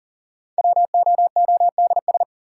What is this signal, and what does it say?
Morse code, heard as a keyed tone, about 40 wpm, 700 Hz. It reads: WOODS
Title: Sentences from Top 1000 Words - Review Part 2 - New Word Drill - 40wpm